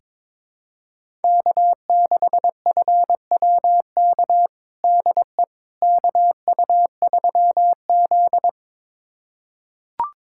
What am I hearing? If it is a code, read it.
Morse code, heard as a keyed tone, about 22 wpm, 700 Hz. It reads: K6FWK DE KU3Z
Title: USA Callsign DE Callsign 22wpm